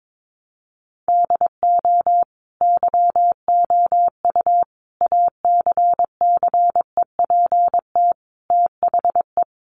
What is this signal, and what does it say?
Morse code, heard as a keyed tone, about 22 wpm, 700 Hz. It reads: DO YOU ACCEPT THE
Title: Sentences from Top 1300 Words 22wpm